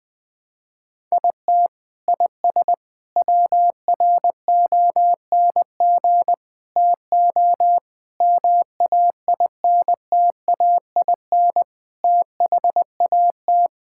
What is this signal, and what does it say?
Morse code, heard as a keyed tone, about 20 wpm, 700 Hz. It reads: IT IS WRONG TO MAINTAIN THAT
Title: Sentences from Top 1000 Words - Review Part 5 - 20wpm